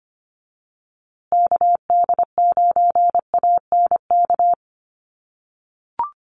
Morse code, heard as a keyed tone, about 25 wpm, 700 Hz. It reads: KD9ANK
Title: USA Call Signs Repeated 3x 25wpm